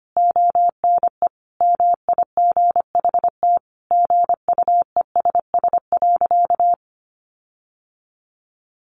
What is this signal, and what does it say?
Morse code, heard as a keyed tone, about 25 wpm, 700 Hz. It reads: ONE MIGHT GUESS.
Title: Sentences from Top 1500 Words 25wpm